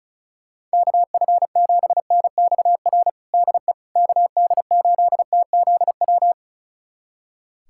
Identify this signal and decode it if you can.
Morse code, heard as a keyed tone, about 36 wpm, 700 Hz. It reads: KF7NXR DE KD8TZW
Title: USA Callsign DE Callsign - 6 Characters - 35wpm